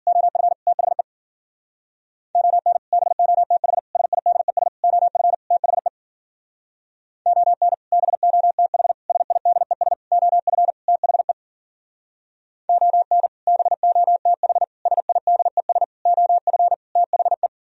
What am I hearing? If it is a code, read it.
Morse code, heard as a keyed tone, about 49 wpm, 700 Hz. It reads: OF THE ON BOTH SIDES OF THE ON BOTH SIDES OF THE ON BOTH SIDES OF THE
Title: Speed-Racing - Sets of 5 Words 40wpm